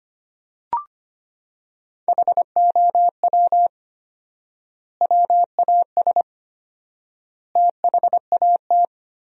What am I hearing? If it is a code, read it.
Morse code, heard as a keyed tone, about 25 wpm, 700 Hz. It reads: HOW WAS THAT
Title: 4x Word-Spacing - Sentences from Top 500 Words - 25wpm